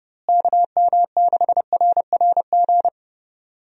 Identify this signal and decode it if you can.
Morse code, heard as a keyed tone, about 32 wpm, 700 Hz. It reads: KM6RRG
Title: USA Call Signs - 6 Characters - Repeated 2x - 30wpm